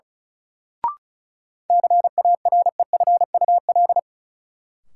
Morse code, heard as a keyed tone, about 36 wpm, 700 Hz. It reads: CAREFUL
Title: Sentences from Top 1600 Words - New Word Drill - 35wpm